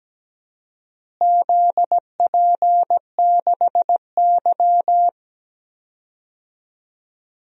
Morse code, heard as a keyed tone, about 17 wpm, 700 Hz. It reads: ZP6Y
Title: DXCC-100 Call Signs - 4 Characters - Repeated 2x - 17wpm